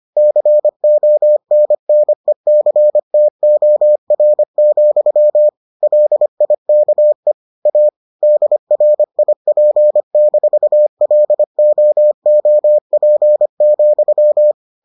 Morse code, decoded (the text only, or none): CONNECTOR, LIKE A DRIP-LOOP,